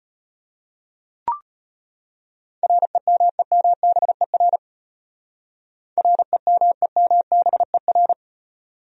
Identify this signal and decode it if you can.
Morse code, heard as a keyed tone, about 35 wpm, 700 Hz. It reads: REMEMBER REMEMBER